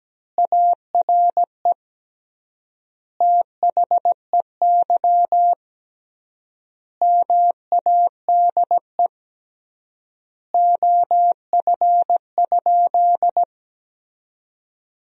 Morse code, heard as a keyed tone, about 17 wpm, 700 Hz. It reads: ARE THEY MADE OF?
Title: 3x Word-Spacing - Sentences from Top 200 Words - 17wpm